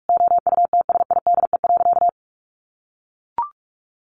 Morse code, heard as a keyed tone, about 44 wpm, 700 Hz. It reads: OUTSIDE.
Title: Sentences from Top 1300 Words 45wpm